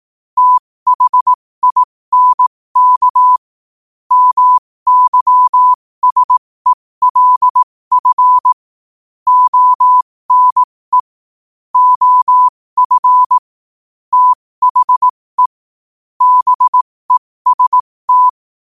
Morse code, heard as a keyed tone, about 18 wpm, spaced as for 15 wpm, 1000 Hz. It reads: THINK MYSELF ONE OF THE BEST
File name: qotdfiles / 15wpm